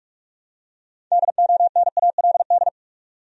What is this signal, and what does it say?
Morse code, heard as a keyed tone, about 46 wpm, 700 Hz. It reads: DONALD
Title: USA Common Names - 45wpm